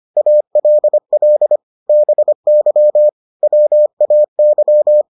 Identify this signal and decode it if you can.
Morse code, heard as a keyed tone, about 25 wpm, 600 Hz. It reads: ALL BY WAY